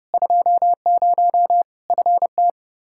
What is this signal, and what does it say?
Morse code, heard as a keyed tone, about 30 wpm, 700 Hz. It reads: 20 FT
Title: Practice QSO - Exchange 2 - Info - 30wpm